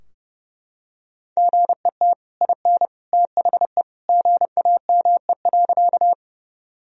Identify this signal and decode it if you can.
Morse code, heard as a keyed tone, about 30 wpm, 700 Hz. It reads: GET IN THE GAME.
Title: Sentences from Top 1000 Words - Review Part 3 - 30wpm